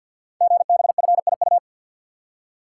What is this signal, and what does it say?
Morse code, heard as a keyed tone, about 52 wpm, 700 Hz. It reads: GBFIU
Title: Code Groups - 5 Letters - 50wpm